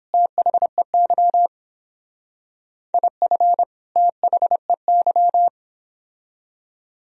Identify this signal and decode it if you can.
Morse code, heard as a keyed tone, about 30 wpm, 700 Hz. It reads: THEY IF THEY